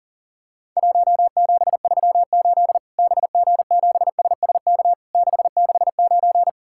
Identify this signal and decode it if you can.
Morse code, heard as a keyed tone, about 41 wpm, 700 Hz. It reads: 1738 BG7SSK 669
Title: Worked All Europe DX Contest - QTC Exchange - 40wpm